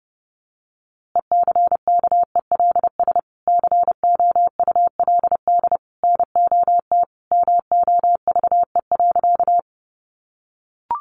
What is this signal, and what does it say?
Morse code, heard as a keyed tone, about 30 wpm, 700 Hz. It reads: ECKELS COULD NOT MOVE.
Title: A Sound Of Thunder 30wpm